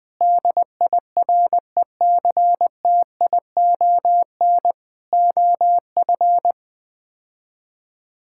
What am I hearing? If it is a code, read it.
Morse code, heard as a keyed tone, about 20 wpm, 700 Hz. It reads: DIRECTION OF
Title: Build A Phrase - 5-Word Phrases - 20wpm